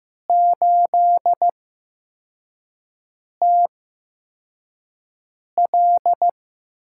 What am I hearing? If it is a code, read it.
Morse code, heard as a keyed tone, about 15 wpm, spaced as for 4 wpm, 700 Hz. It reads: 8TL
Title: USA Call Signs - 4 Characters - 8x Character Spacing - 15wpm